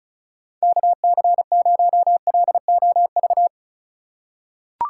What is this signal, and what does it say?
Morse code, heard as a keyed tone, about 34 wpm, 700 Hz. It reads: KC0LOV
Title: USA Call Signs - 6 Characters - 3x Character Spacing - 35wpm